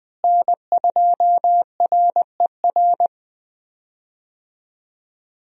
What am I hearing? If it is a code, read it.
Morse code, heard as a keyed tone, about 20 wpm, 700 Hz. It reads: N2RER
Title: USA Call Signs - 5 Characters - Repeated 2x - 20wpm